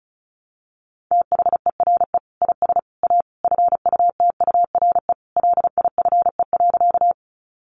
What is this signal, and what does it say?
Morse code, heard as a keyed tone, about 34 wpm, 700 Hz. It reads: THERE IS A FUTURE LIFE.
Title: Sentences from Top 700 Words 35wpm